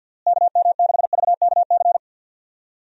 Morse code, heard as a keyed tone, about 46 wpm, 700 Hz. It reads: KM6VKC
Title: USA Call Signs - 6 Characters - Repeated 2x - 50wpm